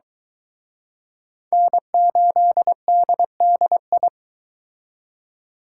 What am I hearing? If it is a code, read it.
Morse code, heard as a keyed tone, about 23 wpm, 700 Hz. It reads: N8DDI